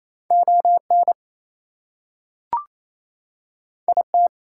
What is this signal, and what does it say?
Morse code, heard as a keyed tone, about 28 wpm, 700 Hz. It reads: ON IT
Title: Two Letter Words - 28wpm